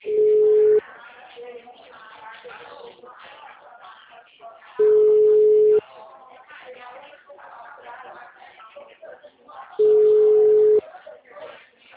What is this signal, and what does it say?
Ringback tone (Europe, 425 Hz), with speech.
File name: out- -2003- - - .18123